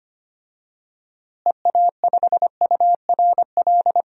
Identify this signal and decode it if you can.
Morse code, heard as a keyed tone, about 25 wpm, 700 Hz. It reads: EA5URL
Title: DXCC-100 Call Signs - 6 Characters - Repeated 2x - 25wpm